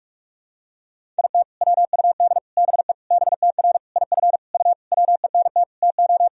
Morse code, heard as a keyed tone, about 46 wpm, 700 Hz. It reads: IT WUD BE BTR IF U WENT TO
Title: Sentences from Top 400 Words and Abbreviations 45wpm